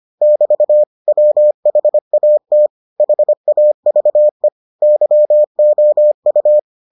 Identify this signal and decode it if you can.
Morse code, heard as a keyed tone, about 25 wpm, 600 Hz. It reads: <BT> WHAT HAVE YOU